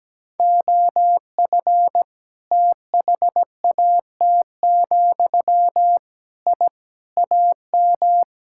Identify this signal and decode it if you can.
Morse code, heard as a keyed tone, about 17 wpm, 700 Hz. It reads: OF THAT, I AM
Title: Sentences from Top 1000 Words - Review Part 5 - 17wpm